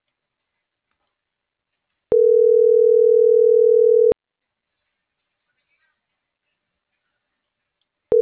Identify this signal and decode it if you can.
Ringback tone (North America), 440 + 480 Hz.